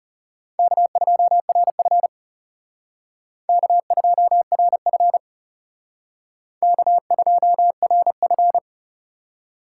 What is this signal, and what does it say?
Morse code, heard as a keyed tone, about 40 wpm, 700 Hz. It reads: K2RF K2RF K2RF